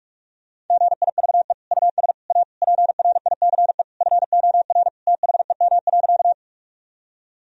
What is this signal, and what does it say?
Morse code, heard as a keyed tone, about 41 wpm, 700 Hz. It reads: GIVE US A PRICE FOR THEM.
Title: Sentences from Top 1000 Words - Review Part 4 - 45wpm